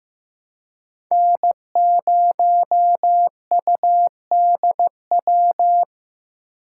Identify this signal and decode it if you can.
Morse code, heard as a keyed tone, about 15 wpm, 700 Hz. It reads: N0UDW
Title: USA Call Signs - 5 Characters - 3x Character Spacing - 15wpm